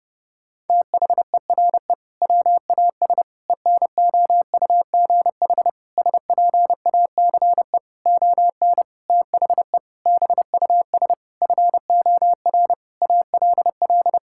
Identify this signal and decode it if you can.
Morse code, heard as a keyed tone, about 32 wpm, 700 Hz. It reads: THERE WAS ENOUGH SPACE ON THE BUS FOR ALL
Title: Sentences from Top 1000 Words - Review Part 6 - 30wpm